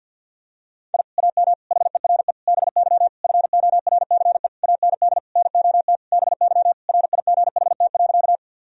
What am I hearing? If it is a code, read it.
Morse code, heard as a keyed tone, about 51 wpm, 700 Hz. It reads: I AM HERE BY FORCE AND NOT BY RIGHT.
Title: Sentences from Top 1000 Words - Review Part 3 - 50wpm